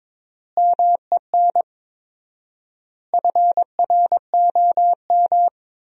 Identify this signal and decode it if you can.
Morse code, heard as a keyed tone, about 21 wpm, 700 Hz. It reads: MEN FROM
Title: 4x Word-Spacing - Sentences from Top 300 Words - 22wpm